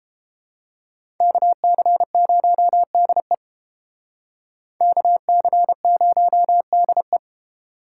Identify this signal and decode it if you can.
Morse code, heard as a keyed tone, about 28 wpm, 700 Hz. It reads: KC0DE KC0DE